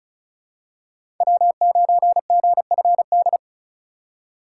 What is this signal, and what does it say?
Morse code, heard as a keyed tone, about 34 wpm, 700 Hz. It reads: W9GFD
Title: USA Call Signs - 5 Characters - 3x Character Spacing - 35wpm